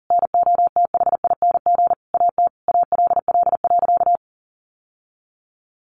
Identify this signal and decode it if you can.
Morse code, heard as a keyed tone, about 40 wpm, 700 Hz. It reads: NOTHING AT ALL.
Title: Sentences from Top 1500 Words 40wpm